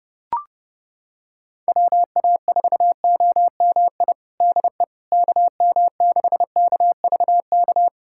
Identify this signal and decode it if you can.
Morse code, heard as a keyed tone, about 30 wpm, 700 Hz. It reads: WA4OMI DE KM6KVK
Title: USA Callsign DE Callsign - 6 Characters - 30wpm